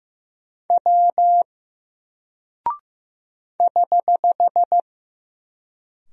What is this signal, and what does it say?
Morse code, heard as a keyed tone, about 15 wpm, 700 Hz. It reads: W <HH>
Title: Single Character (Letter:Number:Punctuation:Prosign) Alt 15wpm